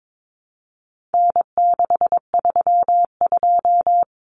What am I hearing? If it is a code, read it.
Morse code, heard as a keyed tone, about 22 wpm, 700 Hz. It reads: N632
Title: Common Cut Numbers - 4-Digit 22wpm